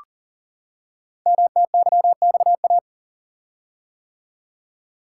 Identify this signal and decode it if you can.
Morse code, heard as a keyed tone, about 41 wpm, 700 Hz. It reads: MTYXA
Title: Code Groups - 5 Letters - 40wpm